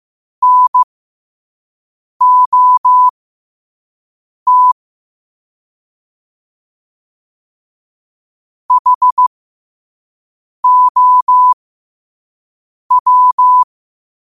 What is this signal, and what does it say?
Morse code, heard as a keyed tone, about 15 wpm, spaced as for 5 wpm, 1000 Hz. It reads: NOT HOW
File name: qotdfiles / 5wpm